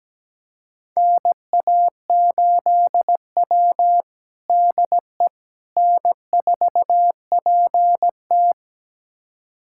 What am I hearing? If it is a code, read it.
Morse code, heard as a keyed tone, about 17 wpm, 700 Hz. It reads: NA8W DE N4PT
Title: USA Callsign DE Callsign - 4 Characters - 17wpm